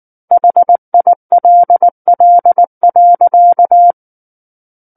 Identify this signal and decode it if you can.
Morse code, heard as a keyed tone, about 19 wpm, 700 Hz. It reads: HILL.
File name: Book One / 10-Chapter 10 - 19 WPM